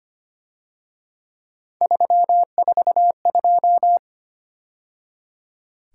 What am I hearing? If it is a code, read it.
Morse code, heard as a keyed tone, about 30 wpm, 700 Hz. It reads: 342
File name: Speed-Racing - Three Letter-Number Combinations - 25wpm